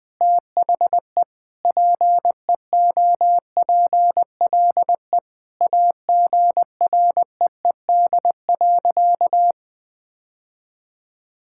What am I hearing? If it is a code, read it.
Morse code, heard as a keyed tone, about 21 wpm, 700 Hz. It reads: THE PEOPLE AGREED.
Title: Sentences with 3-Word Phrases - Spoken Beginning - 20wpm